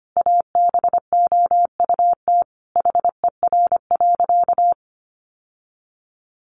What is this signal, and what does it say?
Morse code, heard as a keyed tone, about 25 wpm, 700 Hz. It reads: ABOUT HER.
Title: Sentences from Top 800 Words 25wpm